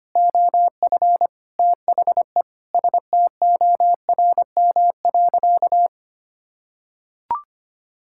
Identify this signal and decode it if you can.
Morse code, heard as a keyed tone, about 25 wpm, 700 Hz. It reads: OF THE STORM.
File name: practice / 25wpm / Sentences with 4-Word Phrases - 25wpm